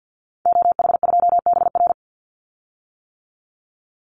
Morse code, heard as a keyed tone, about 50 wpm, 700 Hz. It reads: OH2BR
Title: Call Signs - 50wpm